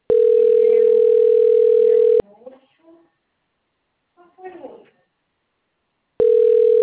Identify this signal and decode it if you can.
Ringback tone (North America), 440 + 480 Hz, with speech.
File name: rg-1600-unknown- - - .8847